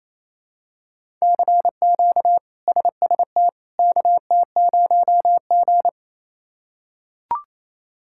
Morse code, heard as a keyed tone, about 28 wpm, 700 Hz. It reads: CQ SST KT0G
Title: K1USN SST QSO Practice - 28wpm